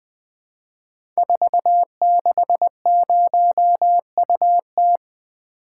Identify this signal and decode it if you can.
Morse code, heard as a keyed tone, about 20 wpm, 700 Hz. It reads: 460UT
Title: Code Groups - 5 Letter:Numbers - 20wpm